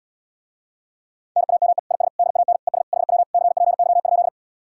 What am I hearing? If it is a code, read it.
Morse code, heard as a keyed tone, about 79 wpm, 700 Hz. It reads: NAME IS DAN ES HW CPY?